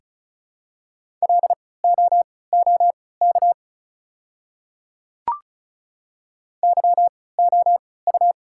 Morse code, heard as a keyed tone, about 36 wpm, spaced as for 20 wpm, 700 Hz. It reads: LOOK YOU
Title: 3x Character Spacing - Top 200 Words - 35wpm